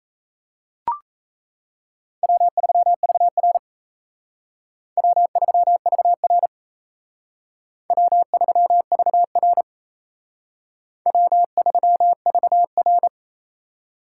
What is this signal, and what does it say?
Morse code, heard as a keyed tone, about 35 wpm, 700 Hz. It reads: W3VR W3VR W3VR W3VR